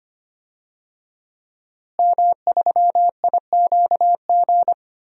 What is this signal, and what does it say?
Morse code, heard as a keyed tone, about 25 wpm, 700 Hz. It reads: M3IQG
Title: DXCC-100 Call Signs - 5 Characters - Repeated 2x - 25wpm